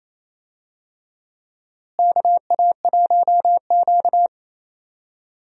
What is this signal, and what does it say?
Morse code, heard as a keyed tone, about 27 wpm, 700 Hz. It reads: KA1Q
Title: USA Call Signs - 4 Characters - Repeated 2x - 28wpm